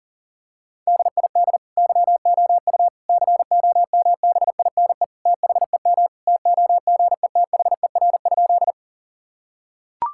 Morse code, heard as a keyed tone, about 37 wpm, 700 Hz. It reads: DID YOU COMBINE THEM TOGETHER?
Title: Sentences from Top 1600 Words 40wpm